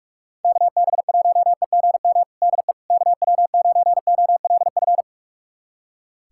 Morse code, heard as a keyed tone, about 44 wpm, 700 Hz. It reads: KB1EGM DE KW9OLF
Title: USA Callsign DE Callsign 45wpm